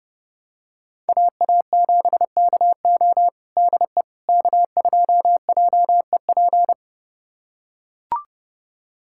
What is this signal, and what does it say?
Morse code, heard as a keyed tone, about 30 wpm, 700 Hz. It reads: AA7KO DE K2JEP
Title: USA Callsign DE Callsign - 5 Characters - 30wpm